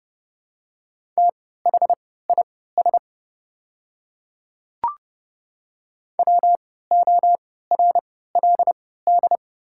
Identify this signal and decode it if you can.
Morse code, heard as a keyed tone, about 30 wpm, spaced as for 17 wpm, 700 Hz. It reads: THIS WORLD
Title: 3x Character Spacing - Top 300 Words - 30wpm